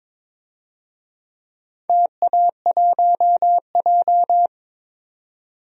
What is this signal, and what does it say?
Morse code, heard as a keyed tone, about 22 wpm, 700 Hz. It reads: TA1J
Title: DXCC-100 Call Signs - 4 Characters - 22wpm